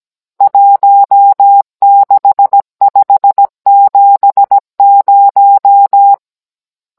Morse code, heard as a keyed tone, about 17 wpm, 800 Hz. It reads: 16570
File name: Cours-NR19-16mm